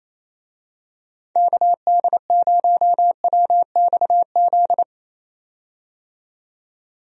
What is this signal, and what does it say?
Morse code, heard as a keyed tone, about 28 wpm, 700 Hz. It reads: KD0WXZ
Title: USA Call Signs - 6 Characters - 28wpm